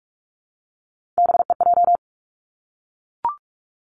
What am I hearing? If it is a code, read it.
Morse code, heard as a keyed tone, about 46 wpm, 700 Hz. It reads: BEJ
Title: Three Challenging Letter-Number Combinations 45wpm